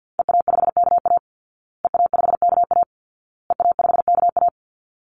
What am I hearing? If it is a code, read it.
Morse code, heard as a keyed tone, about 48 wpm, 700 Hz. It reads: EA5KA EA5KA EA5KA
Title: Call Signs Repeated 2x - 50wpm